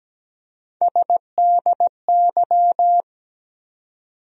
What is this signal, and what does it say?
Morse code, heard as a keyed tone, about 17 wpm, 700 Hz. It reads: SDY
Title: USA Call Sign Postfixes - 3 Letters 17wpm